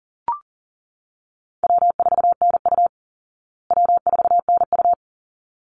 Morse code, heard as a keyed tone, about 40 wpm, 700 Hz. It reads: W4NU W4NU
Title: Easy Call Signs Repeated 2x - 40wpm